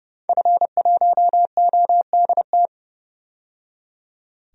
Morse code, heard as a keyed tone, about 30 wpm, 700 Hz. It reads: F1ODT
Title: DXCC-100 Call Signs - 5 Characters - 30wpm